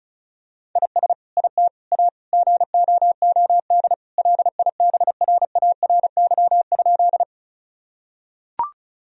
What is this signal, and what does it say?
Morse code, heard as a keyed tone, about 36 wpm, 700 Hz. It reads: IS IT A GOOD LIBRARY?
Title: Sentences from Top 1000 Words - Review Part 4 - 35wpm